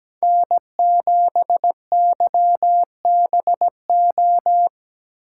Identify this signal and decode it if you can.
Morse code, heard as a keyed tone, about 17 wpm, 700 Hz. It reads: N7YBO
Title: USA Call Signs - 5 Characters - 3x Character Spacing - 17wpm